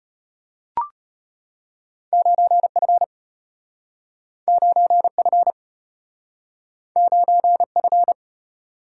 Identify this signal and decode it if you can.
Morse code, heard as a keyed tone, about 34 wpm, 700 Hz. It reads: 9F 9F 9F